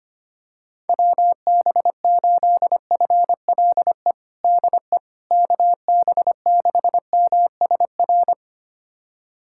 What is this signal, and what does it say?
Morse code, heard as a keyed tone, about 25 wpm, 700 Hz. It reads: WB8FLE DE KB6MSR
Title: USA Callsign DE Callsign - 6 Characters - 25wpm